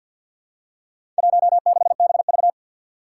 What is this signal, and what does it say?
Morse code, heard as a keyed tone, about 49 wpm, 700 Hz. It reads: 16BV
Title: Four Challenging Letter-Number Combinations 50wpm